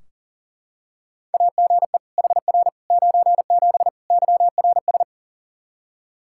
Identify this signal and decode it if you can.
Morse code, heard as a keyed tone, about 40 wpm, 700 Hz. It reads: AGE HR 97 YRS
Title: Practice QSO - Exchange 3 - More Info - 40wpm